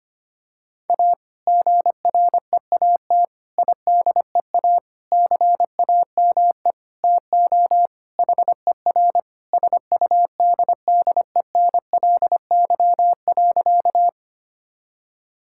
Morse code, heard as a keyed tone, about 25 wpm, 700 Hz. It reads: A GREAT IDEA CAME TO HER SUDDENLY.
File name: practice / 25wpm / Sentences with 2-Word Phrases - 25wpm